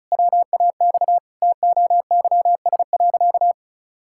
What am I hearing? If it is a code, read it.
Morse code, heard as a keyed tone, about 34 wpm, 700 Hz. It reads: WAX TOYS.
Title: Pangram Challenge v2 - 35wpm